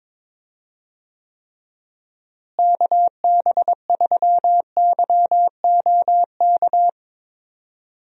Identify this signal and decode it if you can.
Morse code, heard as a keyed tone, about 22 wpm, 700 Hz. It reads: KB3YOK
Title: USA Call Signs - 6 Characters - Repeated 3x - 22wpm